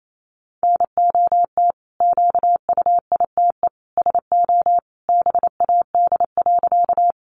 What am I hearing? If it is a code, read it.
Morse code, heard as a keyed tone, about 28 wpm, 700 Hz. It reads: NOT QUITE SO BAD.
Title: Sentences from Top 700 Words 28wpm